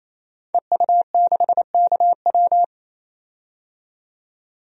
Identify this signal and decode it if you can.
Morse code, heard as a keyed tone, about 28 wpm, 700 Hz. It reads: EU6KW
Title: DXCC-100 Call Signs - 5 Characters - 28wpm